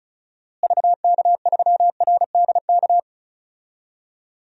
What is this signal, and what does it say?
Morse code, heard as a keyed tone, about 36 wpm, 700 Hz. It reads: VK3RDK
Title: DXCC-100 Call Signs - 6 Characters - Repeated 2x - 35wpm